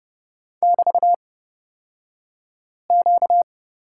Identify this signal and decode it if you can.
Morse code, heard as a keyed tone, about 30 wpm, spaced as for 5 wpm, 700 Hz. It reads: <BT>Q
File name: Lesson 38d - Letter q - Two Characters 30wpm